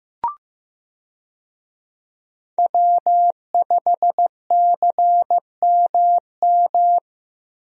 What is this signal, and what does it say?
Morse code, heard as a keyed tone, about 15 wpm, 700 Hz. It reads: W5CMM
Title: USA Call Signs - 5 Characters - Repeated 2x - 15wpm